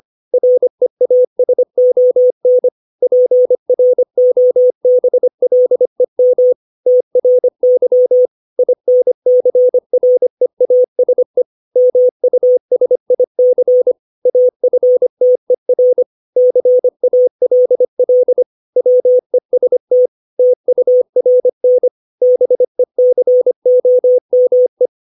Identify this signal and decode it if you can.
Morse code, heard as a keyed tone, about 25 wpm, 500 Hz. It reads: REASON PROBLEM TRY INCREASE MUSIC AFTER CALL WEST TURN BECOME
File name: 632 words / 632 Words 25WPM